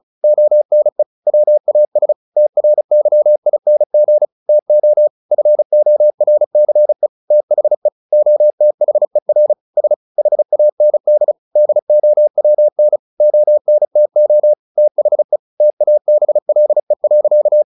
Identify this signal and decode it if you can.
Morse code, heard as a keyed tone, about 34 wpm, 600 Hz. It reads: ONE WAS TRYING TO FORCE THE OTHER S HAND DOWN ONTO THE TABLE.
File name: the old man and the sea / the old man and the sea 1 35wpm